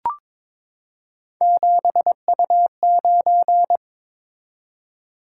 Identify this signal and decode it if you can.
Morse code, heard as a keyed tone, about 22 wpm, 700 Hz. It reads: 7U9
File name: practice / 22wpm / Three Letter-Number Combinations - 22wpm